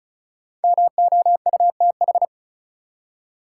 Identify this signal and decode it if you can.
Morse code, heard as a keyed tone, about 36 wpm, 700 Hz. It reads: MOUTH
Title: Five Letter Words - 35wpm